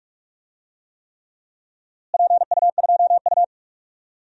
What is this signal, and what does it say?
Morse code, heard as a keyed tone, about 43 wpm, 700 Hz. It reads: PU2U
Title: DXCC-100 Call Signs - 4 Characters - Repeated 3x - 45wpm